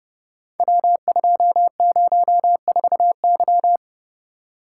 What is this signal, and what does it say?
Morse code, heard as a keyed tone, about 30 wpm, 700 Hz. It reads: W204Y
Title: Code Groups - 5 Letter:Numbers - 30wpm